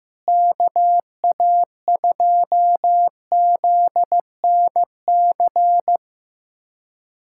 Morse code, heard as a keyed tone, about 16 wpm, 700 Hz. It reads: KA2ZNC
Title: USA Callsign DE Callsign - 6 Characters - 15wpm